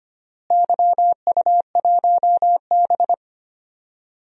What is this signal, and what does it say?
Morse code, heard as a keyed tone, about 25 wpm, 700 Hz. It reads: YU1B
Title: DXCC-100 Call Signs - 4 Characters - Repeated 3x - 25wpm